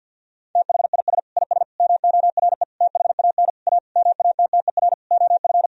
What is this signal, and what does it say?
Morse code, heard as a keyed tone, about 50 wpm, 700 Hz. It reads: THIS IS MORE THAN A MATTER OF
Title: Sentences from Top 1000 Words - Review Part 3 - 50wpm